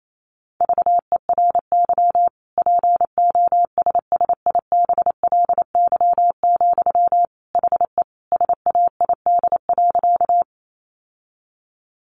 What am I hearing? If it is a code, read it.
Morse code, heard as a keyed tone, about 28 wpm, 700 Hz. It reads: VERY POSSIBLY, HE SAID.
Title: Sentences from Top 1300 Words 28wpm